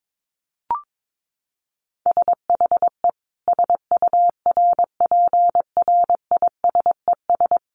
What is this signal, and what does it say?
Morse code, heard as a keyed tone, about 22 wpm, 700 Hz. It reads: SHE SURPRISES
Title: Additional Easy Phrases 22wpm